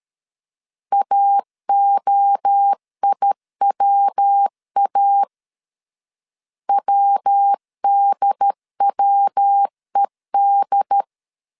Morse code, heard as a keyed tone, about 13 wpm, 800 Hz. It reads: AOIWA WDWED
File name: CW / DET / 1 3 1500C S C 12wpm 2750